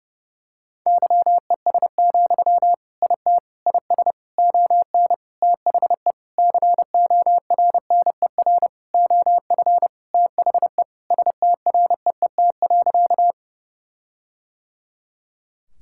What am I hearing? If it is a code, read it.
Morse code, heard as a keyed tone, about 32 wpm, 700 Hz. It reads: YES, IT IS ON THE CORNER OF THE STREET.
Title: Sentences from Top 1600 Words 30wpm